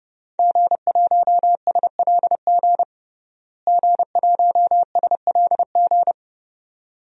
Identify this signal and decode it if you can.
Morse code, heard as a keyed tone, about 30 wpm, 700 Hz. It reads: G1SLG G1SLG
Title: DXCC-100 Call Signs - 5 Characters - Repeated 2x - 30wpm